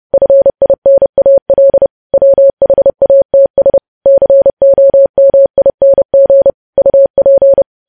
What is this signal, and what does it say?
Morse code, heard as a keyed tone, about 30 wpm, 550 Hz. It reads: FINAL WHATS COMING UP
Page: news headlines in morse code at 30 wpm